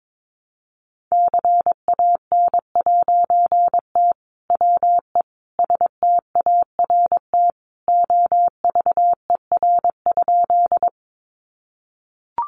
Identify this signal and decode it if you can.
Morse code, heard as a keyed tone, about 22 wpm, 700 Hz. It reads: CAN'T WE START OVER?
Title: A Sound Of Thunder 22wpm